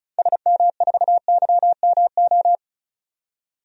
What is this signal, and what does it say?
Morse code, heard as a keyed tone, about 36 wpm, 700 Hz. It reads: SM4YMO
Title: DXCC-100 Call Signs - 6 Characters - 35wpm